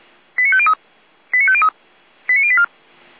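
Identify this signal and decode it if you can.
Five-tone selective call (ZVEI), heard as quick runs of five tones, tones of about 70 ms. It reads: 89503 89503 89084
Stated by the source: Received via SDR as a ZVEI five-tone call, address 89120.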